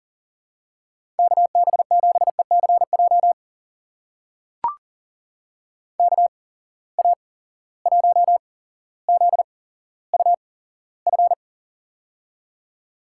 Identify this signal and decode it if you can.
Morse code, heard as a keyed tone, about 38 wpm, spaced as for 17 wpm, 700 Hz. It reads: KB7ECJ KA1ZUF
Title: USA Call Signs - 6 Characters - 8x Character Spacing - 40wpm